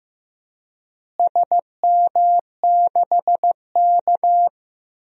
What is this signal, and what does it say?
Morse code, heard as a keyed tone, about 15 wpm, 700 Hz. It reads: SM6K
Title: DXCC-100 Call Signs - 4 Characters - Repeated 2x - 15wpm